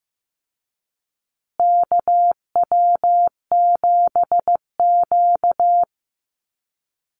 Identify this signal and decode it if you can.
Morse code, heard as a keyed tone, about 15 wpm, 700 Hz. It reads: KW7Q
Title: Call Signs Repeated 3x - 15wpm